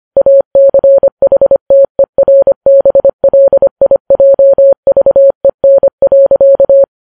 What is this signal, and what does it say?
Morse code, heard as a keyed tone, about 25 wpm, 550 Hz. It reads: ACHTERBLIJVEN.
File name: Files Web tls / Morse on9fv / morse25kps